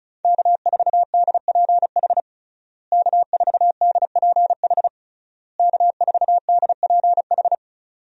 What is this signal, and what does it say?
Morse code, heard as a keyed tone, about 36 wpm, 700 Hz. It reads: K4DPH K4DPH K4DPH
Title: USA Call Signs - 5 Characters - Repeated 3x - 35wpm